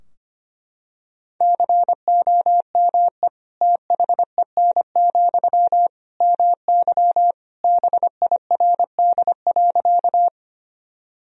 Morse code, heard as a keyed tone, about 25 wpm, 700 Hz. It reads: COME THEN, MY BIRD.
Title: Sentences from Top 1000 Words - Review Part 2 - 25wpm